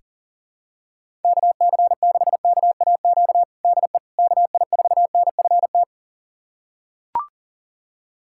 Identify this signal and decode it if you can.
Morse code, heard as a keyed tone, about 40 wpm, 690 Hz. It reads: KC6KAQ DE KI4NFT
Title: USA Callsign DE Callsign - 6 Characters - 40wpm